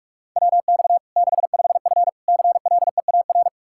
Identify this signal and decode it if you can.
Morse code, heard as a keyed tone, about 41 wpm, 700 Hz. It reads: WX 65F CLEAR
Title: Practice QSO - Exchange 2 - Info - 45wpm